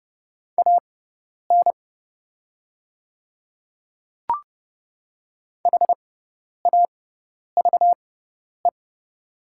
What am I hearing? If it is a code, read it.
Morse code, heard as a keyed tone, about 30 wpm, spaced as for 10 wpm, 700 Hz. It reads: AN HAVE
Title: 6x Character Spacing - Top 100 Words - 30wpm